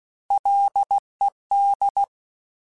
Morse code, heard as a keyed tone, about 16 wpm, 800 Hz. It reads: LED
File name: Alphabetical / Alphabetical mp3s / Ls @15